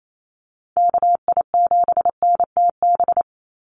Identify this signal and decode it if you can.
Morse code, heard as a keyed tone, about 28 wpm, 700 Hz. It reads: KI7NTB
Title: USA Call Signs 28wpm